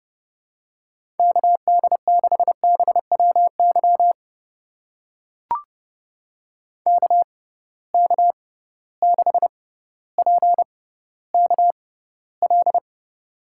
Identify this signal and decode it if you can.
Morse code, heard as a keyed tone, about 30 wpm, spaced as for 15 wpm, 700 Hz. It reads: KD6BWY KK6PKL
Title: USA Call Signs - 6 Characters - 6x Character Spacing - 30wpm